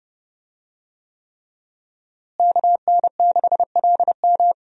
Morse code, heard as a keyed tone, about 32 wpm, 700 Hz. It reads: KN6LM
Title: USA Call Signs - 5 Characters - Repeated 3x - 30wpm